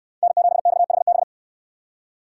Morse code, heard as a keyed tone, about 65 wpm, 700 Hz. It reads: N7XHB